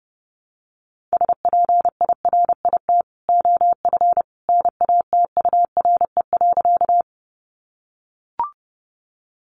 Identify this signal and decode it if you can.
Morse code, heard as a keyed tone, about 30 wpm, 700 Hz. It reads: SPIRIT OF NATURE.
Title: Sentences from Top 1400 Words 30wpm